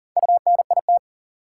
Morse code, heard as a keyed tone, about 40 wpm, 700 Hz. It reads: UNIT
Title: Sentences from Top 1000 Words - Review Part 2 - New Word Drill - 40wpm